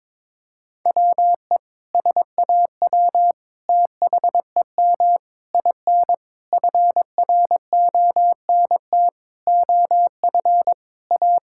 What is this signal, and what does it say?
Morse code, heard as a keyed tone, about 22 wpm, 700 Hz. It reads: WE SAW THEM IN FRONT OF A
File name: practice / 22wpm / Sentences with 4-Word Phrases - 22wpm